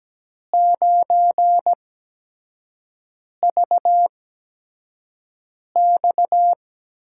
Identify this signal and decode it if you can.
Morse code, heard as a keyed tone, about 17 wpm, spaced as for 5 wpm, 700 Hz. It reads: 9VX
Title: USA Call Signs - 8x Character Spacing - 17wpm